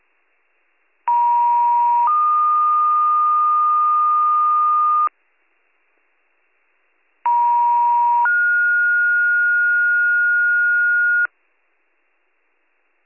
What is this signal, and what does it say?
Two-tone page, heard as a one-second tone, then a three-second tone, about 950 then 1220 Hz, about 950 then 1430 Hz. It is heard from 1.1 to 11.3 s.